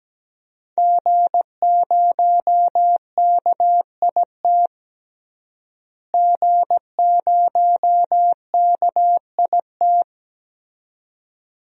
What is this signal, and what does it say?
Morse code, heard as a keyed tone, about 17 wpm, 700 Hz. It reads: G0KIT G0KIT
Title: DXCC-100 Call Signs - 5 Characters - Repeated 2x - 17wpm